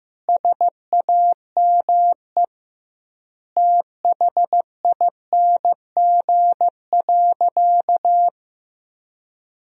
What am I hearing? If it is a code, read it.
Morse code, heard as a keyed tone, about 14 wpm, 700 Hz. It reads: SAME THING.
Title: 2x Word-Spacing - Sentences from Top 200 Words - 15wpm